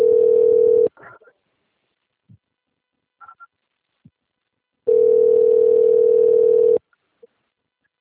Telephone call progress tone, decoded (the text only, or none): ringback tone